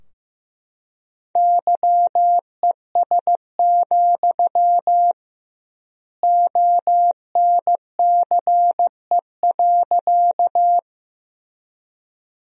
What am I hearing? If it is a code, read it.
Morse code, heard as a keyed tone, about 16 wpm, 700 Hz. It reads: YES, ONCE.
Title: 2x Word-Spacing - Sentences from Top 500 Words - 15wpm